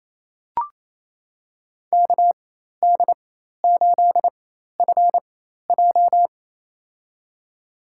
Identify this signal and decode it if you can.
Morse code, heard as a keyed tone, about 27 wpm, spaced as for 13 wpm, 700 Hz. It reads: KD8FJ
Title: USA Call Signs - 4x Character Spacing - 28wpm